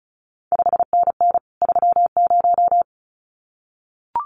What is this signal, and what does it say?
Morse code, heard as a keyed tone, about 36 wpm, 700 Hz. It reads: 5NN 30
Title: CQ WW DX Contest Exchange Encore - 35wpm